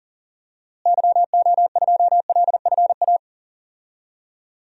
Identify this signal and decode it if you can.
Morse code, heard as a keyed tone, about 40 wpm, 700 Hz. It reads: YO2LFA
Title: DXCC-100 Call Signs - 6 Characters - Repeated 2x - 40wpm